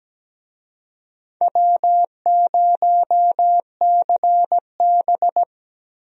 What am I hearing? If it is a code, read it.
Morse code, heard as a keyed tone, about 17 wpm, 700 Hz. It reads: W0CB
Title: USA Call Signs - 4 Characters - Repeated 3x - 17wpm